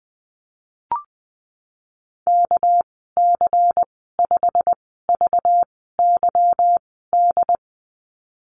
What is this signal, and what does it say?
Morse code, heard as a keyed tone, about 20 wpm, spaced as for 15 wpm, 700 Hz. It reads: KC5VYD
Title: 213 Lesson 34g - Number 8 - Call Signs 20wpm